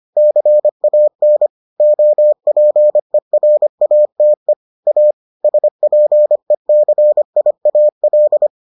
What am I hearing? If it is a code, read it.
Morse code, heard as a keyed tone, about 25 wpm, 600 Hz. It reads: CAN OPERATE A SPECIAL